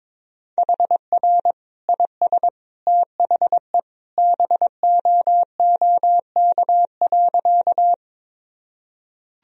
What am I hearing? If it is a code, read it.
Morse code, heard as a keyed tone, about 22 wpm, 700 Hz. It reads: HR IS THE BOOK.
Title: Sentences from Top 300 Words and Abbreviations 22wpm